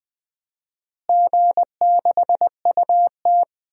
Morse code, heard as a keyed tone, about 20 wpm, 700 Hz. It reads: G6UT
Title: DXCC-100 Call Signs - 4 Characters - Repeated 3x - 20wpm